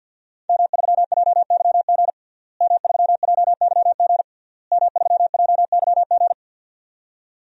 Morse code, heard as a keyed tone, about 53 wpm, 700 Hz. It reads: M3JYG M3JYG M3JYG
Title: DXCC-100 Call Signs - 5 Characters - Repeated 3x - 50wpm